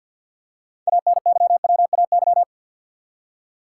Morse code, heard as a keyed tone, about 52 wpm, 700 Hz. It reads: ANYWAY
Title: Six Letter Words - 50wpm